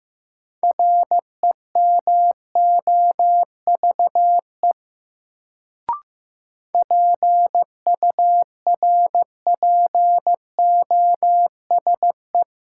Morse code, heard as a keyed tone, about 15 wpm, 700 Hz. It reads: REMOVE PURPOSE
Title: Sentences from Top 800 Words - New Word Drill - 15wpm